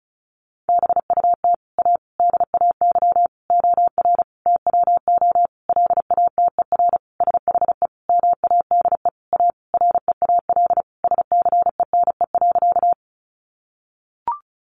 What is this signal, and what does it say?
Morse code, heard as a keyed tone, about 34 wpm, 700 Hz. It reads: BUT A DAY OR TWO LATER SHE MADE A REAL SCENE.
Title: Sentences from Top 1100 Words 35wpm